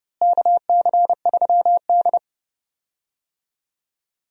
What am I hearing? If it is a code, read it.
Morse code, heard as a keyed tone, about 30 wpm, 700 Hz. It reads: KC3D
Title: USA Call Signs - 4 Characters - Repeated 3x - 30wpm